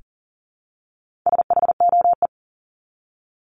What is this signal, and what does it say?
Morse code, heard as a keyed tone, about 40 wpm, 700 Hz. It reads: SHOE
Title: Top 1751-2000 Words 40wpm